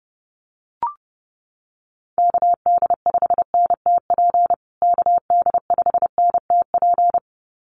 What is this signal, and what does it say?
Morse code, heard as a keyed tone, about 30 wpm, 700 Hz. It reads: KD5NTP KD5NTP
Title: USA Call Signs Repeated 2x 30wpm